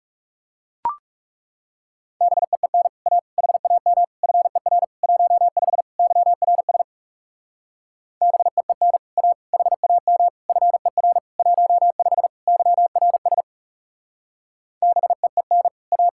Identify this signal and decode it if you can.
Morse code, heard as a keyed tone, about 38 wpm, 700 Hz. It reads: BEEN A HAM FER 15 YRS BEEN A HAM FER 15 YRS BEEN A